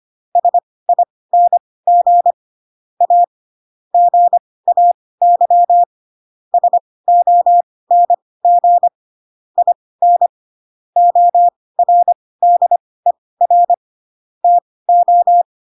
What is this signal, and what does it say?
Morse code, heard as a keyed tone, about 25 wpm, spaced as for 18 wpm, 700 Hz. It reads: SING A GAY SONG IN ORDER TO
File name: Pinocchio / Pinocchio ch1 12 25 ik1hge